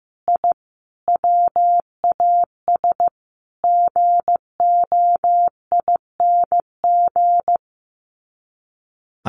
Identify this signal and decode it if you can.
Morse code, heard as a keyed tone, about 15 wpm, 700 Hz. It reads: I WAS GOING
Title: Sets of 5 Words Encore2 15wpm